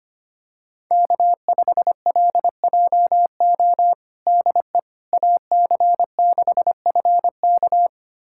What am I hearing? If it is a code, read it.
Morse code, heard as a keyed tone, about 25 wpm, 700 Hz. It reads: K5LJO DE AC6FK
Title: USA Callsign DE Callsign - 5 Characters - 25wpm